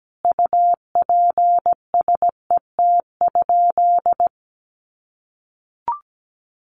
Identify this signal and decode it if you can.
Morse code, heard as a keyed tone, about 17 wpm, 700 Hz. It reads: UPSET?
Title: Sentences from Top 800 Words 17wpm